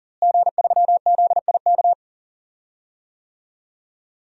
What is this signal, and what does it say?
Morse code, heard as a keyed tone, about 40 wpm, 700 Hz. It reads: G3ZIK
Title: DXCC-100 Call Signs - 5 Characters - Repeated 2x - 40wpm